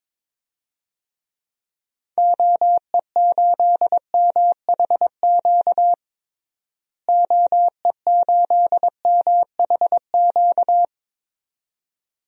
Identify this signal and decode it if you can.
Morse code, heard as a keyed tone, about 22 wpm, 700 Hz. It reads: OE8MHQ OE8MHQ
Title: DXCC-100 Call Signs - 6 Characters - Repeated 2x - 22wpm